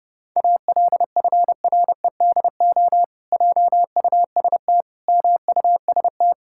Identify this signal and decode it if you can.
Morse code, heard as a keyed tone, about 30 wpm, 700 Hz. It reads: ALFREDO JUST MUST
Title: Pangram Challenge v2 - 30wpm